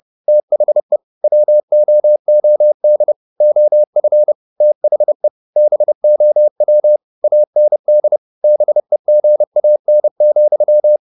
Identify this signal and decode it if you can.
Morse code, heard as a keyed tone, about 30 wpm, 600 Hz. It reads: THE WOOD OF THE BOW AND BEGAN,
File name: the old man and the sea / the old man and the sea 1 30wpm